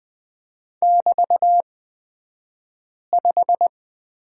Morse code, heard as a keyed tone, about 20 wpm, spaced as for 5 wpm, 700 Hz. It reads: <BT>5
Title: 223 Lesson 36d - BT - Two Characters 20wpm